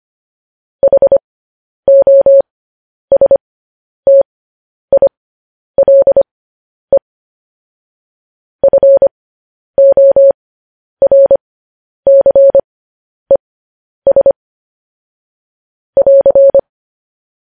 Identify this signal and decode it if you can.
Morse code, heard as a keyed tone, about 25 wpm, spaced as for 10 wpm, 550 Hz. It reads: HOSTILE FORCES <AR>